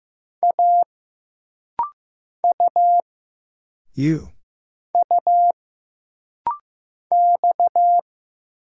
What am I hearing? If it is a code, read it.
Morse code, heard as a keyed tone, about 15 wpm, 700 Hz. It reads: A U U X
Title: Single Letters - Rapid-Fire 15wpm